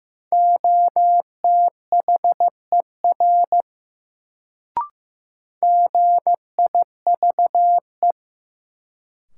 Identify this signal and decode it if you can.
Morse code, heard as a keyed tone, about 15 wpm, 700 Hz. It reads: OTHER GIVE
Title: Sets of 2 Words - New Word Drill - 15wpm